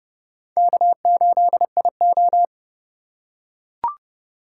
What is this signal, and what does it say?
Morse code, heard as a keyed tone, about 30 wpm, 700 Hz. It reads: K8IO
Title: USA Call Signs - 4 Characters - 2x Character Spacing - 30wpm